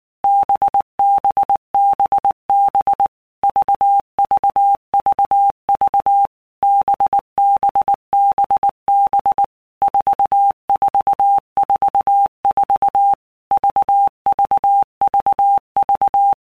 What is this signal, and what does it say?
Morse code, heard as a keyed tone, about 19 wpm, 800 Hz. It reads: BBBB VVVV BBBB 4444 VVVV
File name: Basic Letter Groups By Blocks / Groups@18 / VB46@18